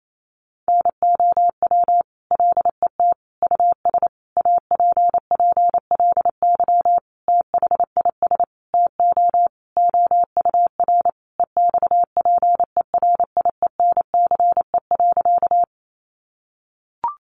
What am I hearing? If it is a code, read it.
Morse code, heard as a keyed tone, about 28 wpm, 700 Hz. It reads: NOW LET US APPLY THIS TO OUR EXPERIENCE.
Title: Sentences from Top 1200 Words 28wpm